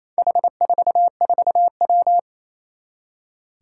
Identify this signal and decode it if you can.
Morse code, heard as a keyed tone, about 28 wpm, 700 Hz. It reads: H44W
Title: Four Challenging Letter-Number Combinations 28wpm